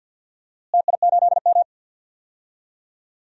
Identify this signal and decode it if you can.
Morse code, heard as a keyed tone, about 47 wpm, 700 Hz. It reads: TI8M
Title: DXCC-100 Call Signs - 4 Characters - Repeated 2x - 50wpm